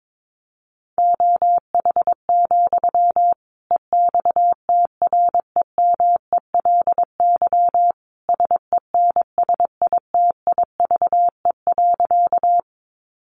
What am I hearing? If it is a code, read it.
Morse code, heard as a keyed tone, about 22 wpm, 700 Hz. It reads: OH, EXTREMELY SENSITIVE.
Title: Sentences from Top 1400 Words 22wpm